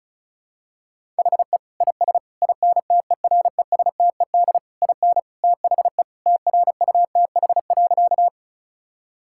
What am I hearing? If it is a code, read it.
Morse code, heard as a keyed tone, about 36 wpm, 700 Hz. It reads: HE IS INTERESTED IN THE TRUTH.
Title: Sentences from Top 1000 Words - Review Part 6 - 35wpm